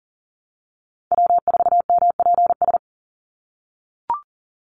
Morse code, heard as a keyed tone, about 40 wpm, 700 Hz. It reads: W4MPS
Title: NA SOTA Chaser Callsigns 40wpm